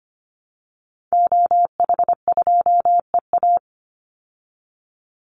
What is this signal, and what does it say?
Morse code, heard as a keyed tone, about 25 wpm, 700 Hz. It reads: OH2EA
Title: Call Signs Repeated 3x - 25wpm